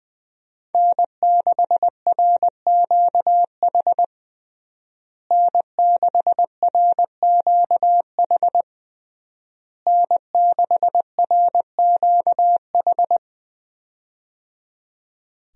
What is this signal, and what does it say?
Morse code, heard as a keyed tone, about 20 wpm, 700 Hz. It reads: N6RQH N6RQH N6RQH
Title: USA Call Signs - 5 Characters - Repeated 3x - 20wpm